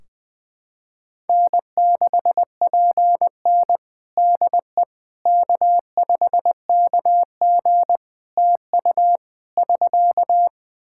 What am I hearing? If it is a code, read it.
Morse code, heard as a keyed tone, about 20 wpm, 700 Hz. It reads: N6PN DE K5KG TU <SK>
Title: Practice QSO - Exchange 4 - Ending - 20wpm